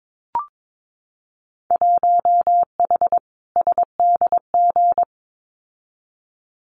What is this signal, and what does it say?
Morse code, heard as a keyed tone, about 22 wpm, 700 Hz. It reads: 1H SDG
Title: Winter Field Day Exchanges 22wpm